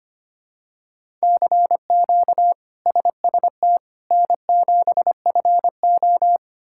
Morse code, heard as a keyed tone, about 25 wpm, 700 Hz. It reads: CQ SST N7FO
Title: K1USN SST QSO Practice - 25wpm